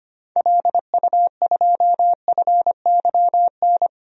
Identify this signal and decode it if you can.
Morse code, heard as a keyed tone, about 25 wpm, 700 Hz. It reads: LU2FYN
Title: DXCC-100 Call Signs - 6 Characters - 25wpm